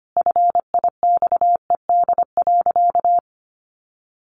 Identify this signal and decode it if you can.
Morse code, heard as a keyed tone, about 25 wpm, 700 Hz. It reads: FIXED.
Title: Sentences from Top 1100 Words 25wpm